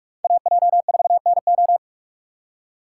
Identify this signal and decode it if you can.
Morse code, heard as a keyed tone, about 44 wpm, 700 Hz. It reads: AJ4NO